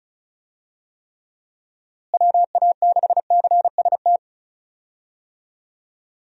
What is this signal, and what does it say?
Morse code, heard as a keyed tone, about 36 wpm, 700 Hz. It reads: WA6CST
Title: USA Call Signs - 6 Characters - Repeated 3x - 35wpm